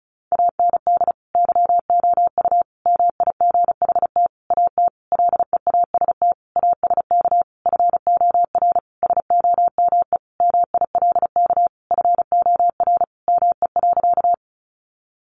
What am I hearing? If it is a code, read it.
Morse code, heard as a keyed tone, about 34 wpm, 700 Hz. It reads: AND YOU MIGHT AT LEAST ASK FOR SOME MILK FOR ME.
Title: Sentences from Top 1500 Words 35wpm